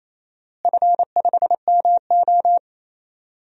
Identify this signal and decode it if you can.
Morse code, heard as a keyed tone, about 28 wpm, 700 Hz. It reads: F5MO
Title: DXCC-100 Call Signs - 4 Characters - Repeated 2x - 28wpm